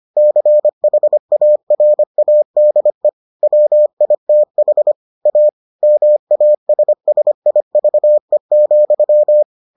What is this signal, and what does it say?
Morse code, heard as a keyed tone, about 25 wpm, 600 Hz. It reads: CHARADE WITH A MASSIVE,